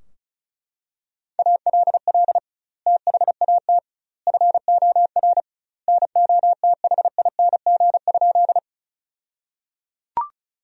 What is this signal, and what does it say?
Morse code, heard as a keyed tone, about 36 wpm, 700 Hz. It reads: ALL THAT FOR NOTHING?
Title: 2x Word-Spacing - Sentences from Top 500 Words - 35wpm